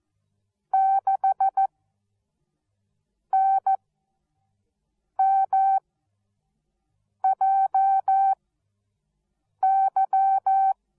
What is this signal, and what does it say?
Morse code, heard as a keyed tone, about 14 wpm, spaced as for 5 wpm, 770 Hz. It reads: 6NMJY